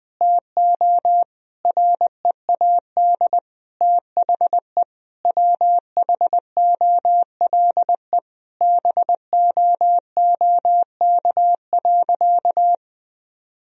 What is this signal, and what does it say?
Morse code, heard as a keyed tone, about 20 wpm, 700 Hz. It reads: TO READ THE WHOLE BOOK.
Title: Sentences from Top 1000 Words - Review Part 1 - 20wpm